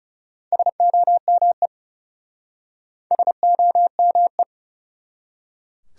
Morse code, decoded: SOME SOME

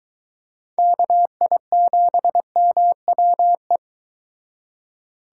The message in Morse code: KI7MWE

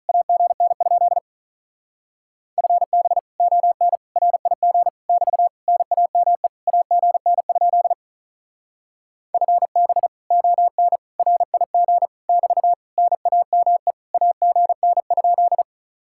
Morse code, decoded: AGN? FB ON RIG <BT> NAME AGN? FB ON RIG <BT> NAME AGN?

